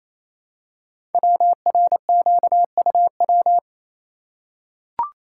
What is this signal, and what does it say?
Morse code, heard as a keyed tone, about 29 wpm, 700 Hz. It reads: WRQUW
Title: Code Groups - 5 Letters - 28wpm